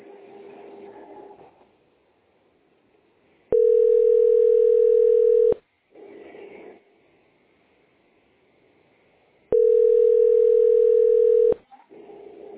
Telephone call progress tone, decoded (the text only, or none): ringback tone